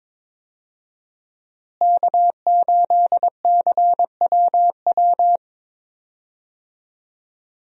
Morse code, heard as a keyed tone, about 22 wpm, 700 Hz. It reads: K8CWW
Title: USA Call Signs - 5 Characters - Repeated 2x - 22wpm